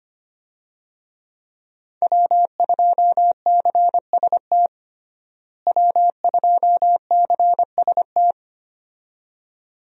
Morse code, heard as a keyed tone, about 25 wpm, 700 Hz. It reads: W2CST W2CST
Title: USA Call Signs - 5 Characters - Repeated 2x - 25wpm